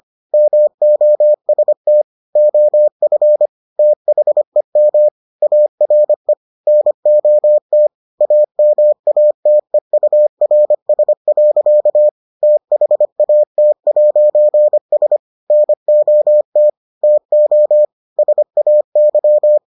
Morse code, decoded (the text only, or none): MOST OF THEM ARE NOT AMATEURS. THAT'S NOT TO SAY